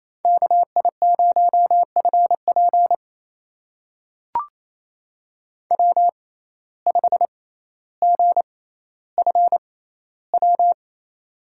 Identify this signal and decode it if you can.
Morse code, heard as a keyed tone, about 27 wpm, spaced as for 14 wpm, 700 Hz. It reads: KI0FP W5GFW
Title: USA Call Signs - 5 Characters - 6x Character Spacing - 28wpm